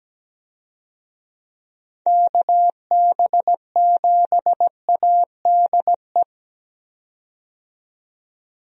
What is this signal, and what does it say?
Morse code, heard as a keyed tone, about 17 wpm, 700 Hz. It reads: KB7ADE
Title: USA Call Signs - 6 Characters - 17wpm